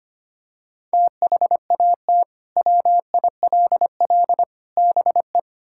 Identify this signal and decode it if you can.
Morse code, heard as a keyed tone, about 25 wpm, 700 Hz. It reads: THAT WILL BE
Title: Sentences from Top 1000 Words - Review Part 5 - 25wpm